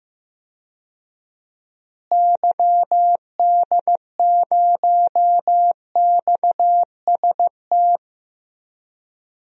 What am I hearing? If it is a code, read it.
Morse code, heard as a keyed tone, about 15 wpm, 700 Hz. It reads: YD0XST
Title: DXCC-100 Call Signs - 6 Characters - Repeated 3x - 15wpm